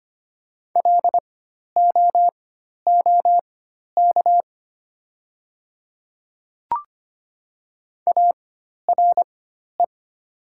Morse code, heard as a keyed tone, about 25 wpm, spaced as for 12 wpm, 700 Hz. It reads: LOOK ARE
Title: 4x Character Spacing - Top 100 Words - 25wpm